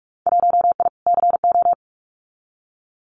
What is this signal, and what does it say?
Morse code, heard as a keyed tone, about 46 wpm, 700 Hz. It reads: 1I CO
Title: Winter Field Day Exchanges Encore 45wpm